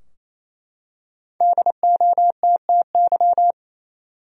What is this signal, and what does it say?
Morse code, heard as a keyed tone, about 28 wpm, 700 Hz. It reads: DOTTY
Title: USA Common Names - Max 5 Letters - 28wpm